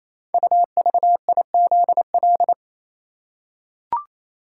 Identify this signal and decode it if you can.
Morse code, heard as a keyed tone, about 28 wpm, 690 Hz. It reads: UVIZL
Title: Code Groups - 5 Letters - 28wpm